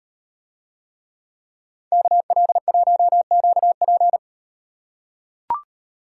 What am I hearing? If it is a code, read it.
Morse code, heard as a keyed tone, about 38 wpm, 700 Hz. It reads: KL1QP